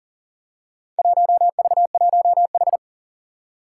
Morse code, heard as a keyed tone, about 40 wpm, 700 Hz. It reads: 1V1H
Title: Four Challenging Letter-Number Combinations 40wpm